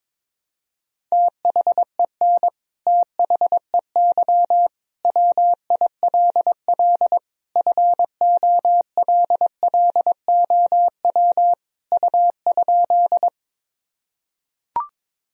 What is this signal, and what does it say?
Morse code, heard as a keyed tone, about 21 wpm, 700 Hz. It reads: THEN THEY WILL FOLLOW U?
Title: Sentences from Top 200 Words and Abbreviations 22wpm